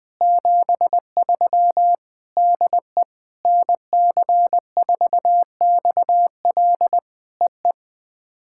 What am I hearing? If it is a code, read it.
Morse code, heard as a keyed tone, about 20 wpm, 700 Hz. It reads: 73 DE NC4XL EE
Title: POTA QSO Practice - 20wpm